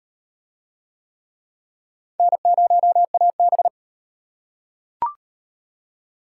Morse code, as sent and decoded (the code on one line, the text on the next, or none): -. ----- .- -...
N0AB